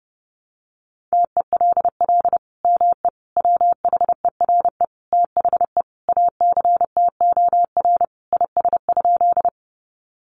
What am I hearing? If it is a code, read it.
Morse code, heard as a keyed tone, about 30 wpm, 700 Hz. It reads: TELL ME WHERE THE ACTOR IS?
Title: Sentences from Top 1500 Words 30wpm